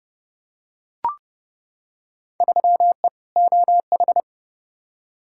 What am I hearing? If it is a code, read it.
Morse code, heard as a keyed tone, about 30 wpm, 700 Hz. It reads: 3E OH
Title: Field Day Exchanges - 30wpm